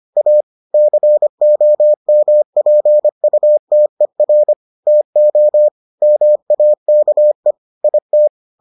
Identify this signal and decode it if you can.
Morse code, heard as a keyed tone, about 25 wpm, 600 Hz. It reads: A COMPUTER TO MAKE IT